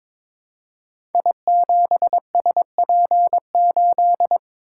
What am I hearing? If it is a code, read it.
Morse code, heard as a keyed tone, about 22 wpm, 700 Hz. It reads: I7SP8
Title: Code Groups - 5 Letter:Numbers - 22wpm